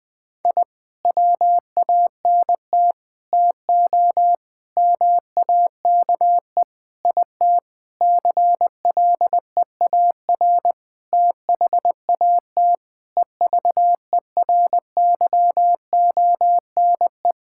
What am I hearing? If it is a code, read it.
Morse code, heard as a keyed tone, about 21 wpm, 700 Hz. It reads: I WANT TO MAKE IT CLEAR THAT EVERYONE
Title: Sentences with 7-Word Phrases - 20wpm